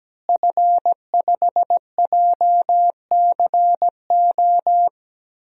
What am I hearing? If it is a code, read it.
Morse code, heard as a keyed tone, about 17 wpm, 700 Hz. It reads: F5JCO
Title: DXCC-100 Call Signs - 5 Characters - Repeated 3x - 17wpm